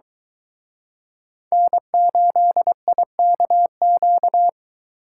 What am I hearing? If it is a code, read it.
Morse code, heard as a keyed tone, about 23 wpm, 700 Hz. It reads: N8IKQ